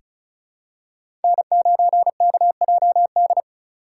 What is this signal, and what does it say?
Morse code, heard as a keyed tone, about 36 wpm, 690 Hz. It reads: N9KJD
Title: USA Call Signs - 3x Character Spacing - 35wpm